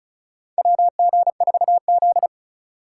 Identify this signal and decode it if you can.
Morse code, heard as a keyed tone, about 36 wpm, 700 Hz. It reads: WG4Z
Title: USA Call Signs - 3x Character Spacing - 35wpm